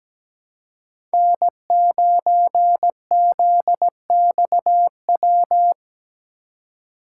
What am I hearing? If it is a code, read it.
Morse code, heard as a keyed tone, about 17 wpm, 700 Hz. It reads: N9ZXW
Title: USA Call Signs - 3x Character Spacing - 17wpm